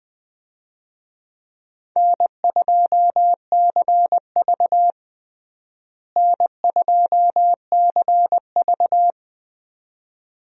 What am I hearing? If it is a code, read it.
Morse code, heard as a keyed tone, about 20 wpm, 700 Hz. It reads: N2CV N2CV
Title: USA Call Signs - 4 Characters - Repeated 2x - 20wpm